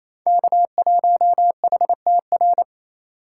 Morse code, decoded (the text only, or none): K1HTR